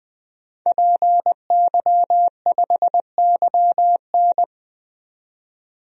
Morse code, heard as a keyed tone, about 21 wpm, 700 Hz. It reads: PY5YN
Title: DXCC-100 Call Signs - 5 Characters - 20wpm